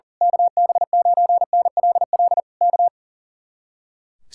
Morse code, heard as a keyed tone, about 38 wpm, 700 Hz. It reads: KB9NLL K
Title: Speed-Racing Common QSO Phrases - 40wpm